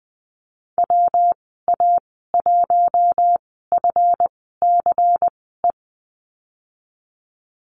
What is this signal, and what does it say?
Morse code, heard as a keyed tone, about 20 wpm, spaced as for 15 wpm, 700 Hz. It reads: WA1FCE
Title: 173 Lesson 28g - Number 9 - Call Signs 20wpm